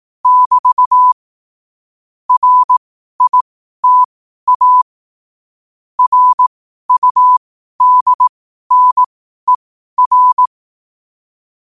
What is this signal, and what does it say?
Morse code, heard as a keyed tone, about 18 wpm, spaced as for 12 wpm, 1000 Hz. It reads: <BT> RITA RUDNER
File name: qotdfiles / 12wpm